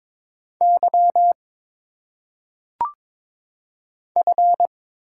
Morse code, heard as a keyed tone, about 22 wpm, 700 Hz. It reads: Y F
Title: Single Letters - Phonetic Pronunciation 22wpm